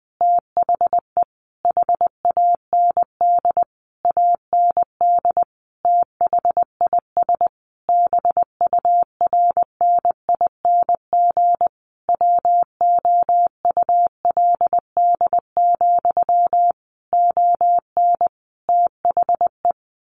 Morse code, heard as a keyed tone, about 20 wpm, 700 Hz. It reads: THE HAND AND THIS BURNING WOULD, ON THE
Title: A Sound Of Thunder 20wpm